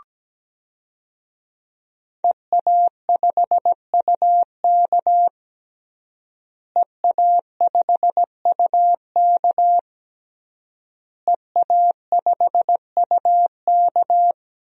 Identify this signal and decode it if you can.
Morse code, heard as a keyed tone, about 17 wpm, 700 Hz. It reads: EA5UK EA5UK EA5UK
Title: DXCC-100 Call Signs - 5 Characters - Repeated 3x - 17wpm